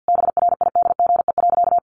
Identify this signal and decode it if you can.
Morse code, heard as a keyed tone, about 50 wpm, 700 Hz. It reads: BRIDGE.
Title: Sentences from Top 1400 Words 50wpm